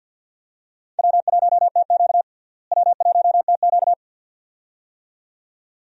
Morse code, heard as a keyed tone, about 46 wpm, 700 Hz. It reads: W1TQ W1TQ
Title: USA Call Signs - 4 Characters - Repeated 2x - 50wpm